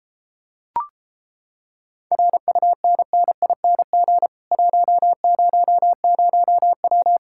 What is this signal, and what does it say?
Morse code, heard as a keyed tone, about 31 wpm, 700 Hz. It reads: RUNNING 100W